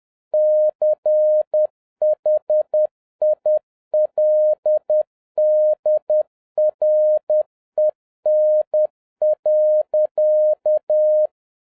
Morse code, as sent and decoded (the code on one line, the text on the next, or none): -.-. .... .. .-.. -.. .-. -. .-.-.-
CHILDREN.